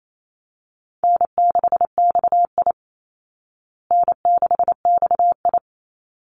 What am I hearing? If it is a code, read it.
Morse code, heard as a keyed tone, about 28 wpm, 700 Hz. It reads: N6XI N6XI
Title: Easy Call Signs Repeated 2x - 28wpm